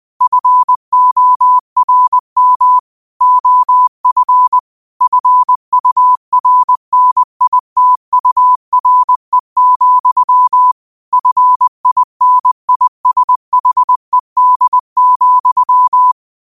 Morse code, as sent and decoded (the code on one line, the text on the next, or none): ..-. --- .-. -- / --- ..-. / ..-. ..- .-. -. .. - ..- .-. --..-- / ..-. .. -. .. ... .... -.. --..--
FORM OF FURNITURE, FINISHED,